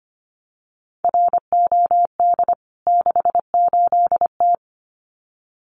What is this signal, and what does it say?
Morse code, heard as a keyed tone, about 25 wpm, 690 Hz. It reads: ROD 68T
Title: CWT Exchange 25wpm